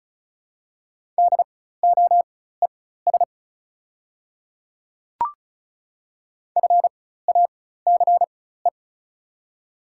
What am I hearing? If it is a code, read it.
Morse code, heard as a keyed tone, about 36 wpm, spaced as for 16 wpm, 700 Hz. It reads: DOES FACE
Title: 4x Character Spacing - Top 300 Words - 35wpm